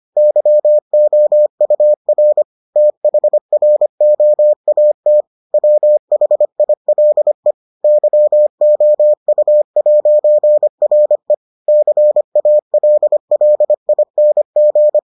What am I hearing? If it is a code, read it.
Morse code, heard as a keyed tone, about 25 wpm, 600 Hz. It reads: YOUR THROAT WHILE YOU'RE CALLING